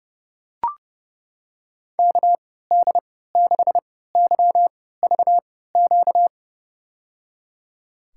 Morse code, heard as a keyed tone, about 32 wpm, spaced as for 17 wpm, 700 Hz. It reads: KD6YVQ
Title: USA Call Signs - 6 Characters - 3x Character Spacing - 30wpm